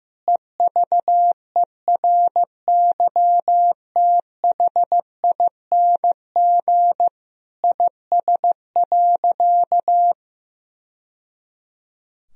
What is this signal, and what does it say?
Morse code, heard as a keyed tone, about 15 wpm, 700 Hz. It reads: EVERYTHING IS.
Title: Sentences from Top 1000 Words - Review Part 4 - 15wpm